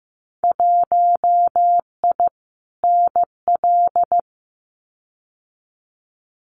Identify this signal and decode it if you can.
Morse code, heard as a keyed tone, about 15 wpm, 700 Hz. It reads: 1I NL
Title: Winter Field Day Exchanges 15wpm